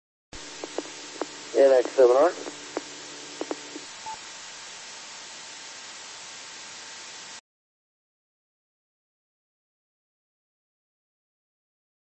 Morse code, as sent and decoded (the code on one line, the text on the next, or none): none